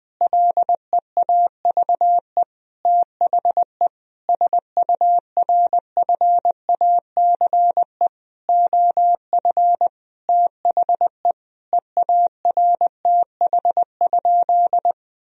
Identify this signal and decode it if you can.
Morse code, heard as a keyed tone, about 20 wpm, 700 Hz. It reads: LEAVE THE SURFACE OF THE EARTH?
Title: Sentences from Top 1000 Words - Review Part 3 - 20wpm